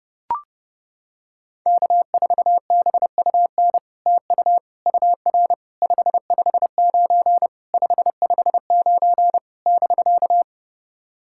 Morse code, heard as a keyed tone, about 30 wpm, 700 Hz. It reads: K4BUN TU UR 559 559 <BK>
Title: SOTA QSO Practice 30wpm